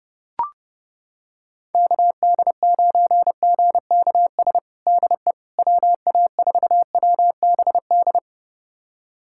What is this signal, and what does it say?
Morse code, heard as a keyed tone, about 30 wpm, 700 Hz. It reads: KD9GKS DE WA4WBD
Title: USA Callsign DE Callsign - 6 Characters - 30wpm